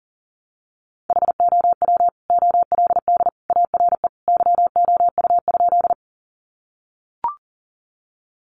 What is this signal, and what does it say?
Morse code, heard as a keyed tone, about 40 wpm, 700 Hz. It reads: HOW OLD ARE YOU?
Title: Sentences from Top 200 Words 40wpm